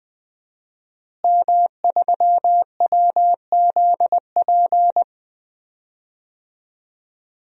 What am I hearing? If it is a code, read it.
Morse code, heard as a keyed tone, about 20 wpm, 700 Hz. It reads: M3WZP
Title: DXCC-100 Call Signs - 5 Characters - Repeated 2x - 20wpm